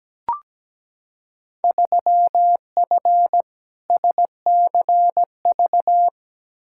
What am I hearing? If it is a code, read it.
Morse code, heard as a keyed tone, about 17 wpm, 700 Hz. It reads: 3F SCV
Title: Field Day Exchanges - 17wpm